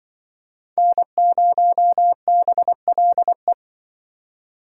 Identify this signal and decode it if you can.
Morse code, heard as a keyed tone, about 24 wpm, 700 Hz. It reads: N0BLE
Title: Speed Racing - USA Call Signs 20wpm